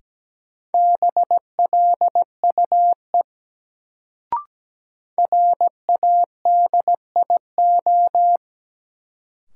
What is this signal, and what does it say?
Morse code, heard as a keyed tone, about 17 wpm, 700 Hz. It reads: BLUE RADIO
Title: Sentences from Top 1000 Words - Review Part 3 - New Word Drill - 17wpm